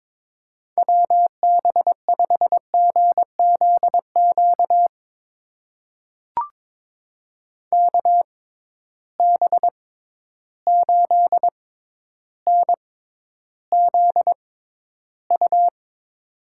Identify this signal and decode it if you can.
Morse code, heard as a keyed tone, about 22 wpm, spaced as for 11 wpm, 700 Hz. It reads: WB5GZQ KB8NZU
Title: USA Call Signs - 6 Characters - 6x Character Spacing - 22wpm